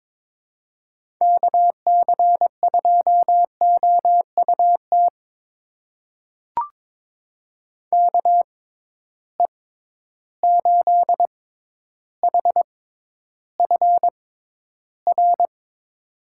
Morse code, heard as a keyed tone, about 22 wpm, spaced as for 11 wpm, 700 Hz. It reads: KC2OUT KE8HFR
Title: USA Call Signs - 6 Characters - 6x Character Spacing - 22wpm